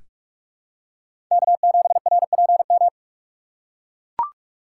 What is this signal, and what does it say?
Morse code, heard as a keyed tone, about 43 wpm, 690 Hz. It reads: K7RPM
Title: USA Call Signs - 5 Characters - 6x Character Spacing - 45wpm